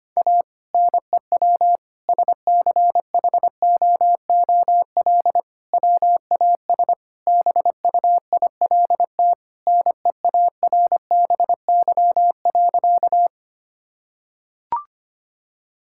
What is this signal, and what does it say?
Morse code, heard as a keyed tone, about 25 wpm, 700 Hz. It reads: A NEW SCHOOL WAS BUILT NEARBY.
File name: practice / 25wpm / Sentences with 2-Word Phrases - 25wpm